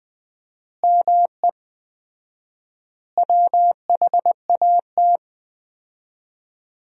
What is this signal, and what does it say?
Morse code, heard as a keyed tone, about 20 wpm, 700 Hz. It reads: ME WHAT
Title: 4x Word-Spacing - Sentences from Top 500 Words - 20wpm